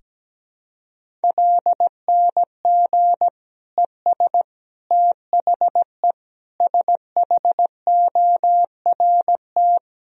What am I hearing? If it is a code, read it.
Morse code, heard as a keyed tone, about 17 wpm, 700 Hz. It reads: LNG ES THE SHORT
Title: Sentences from Top 400 Words and Abbreviations 17wpm